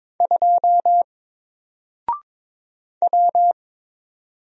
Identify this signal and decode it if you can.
Morse code, heard as a keyed tone, about 22 wpm, 700 Hz. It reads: 2 W
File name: Single-Letter-Number-22wpm